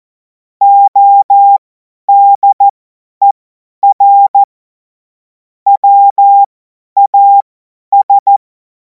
Morse code, heard as a keyed tone, about 14 wpm, spaced as for 10 wpm, 800 Hz. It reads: ODER WAS
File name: Morsekurs / Witze / Witz 03. Februar 2026 50 F70